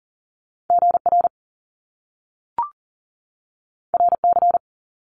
Morse code, heard as a keyed tone, about 40 wpm, 700 Hz. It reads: GR RC
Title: Top Two Letter Combinations in Words 40wpm